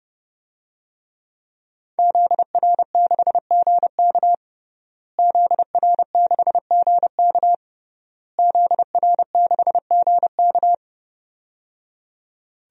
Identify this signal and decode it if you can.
Morse code, heard as a keyed tone, about 28 wpm, 700 Hz. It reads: ZR6GK ZR6GK ZR6GK
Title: DXCC-100 Call Signs - 5 Characters - Repeated 3x - 30wpm